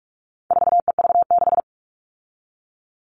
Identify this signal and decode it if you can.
Morse code, heard as a keyed tone, about 44 wpm, 700 Hz. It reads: 4EV6
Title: All Cut Numbers - 4-Digit 45wpm